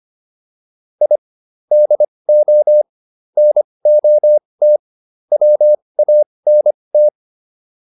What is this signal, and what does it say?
Morse code, heard as a keyed tone, about 25 wpm, spaced as for 20 wpm, 600 Hz. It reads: I DO NOT WANT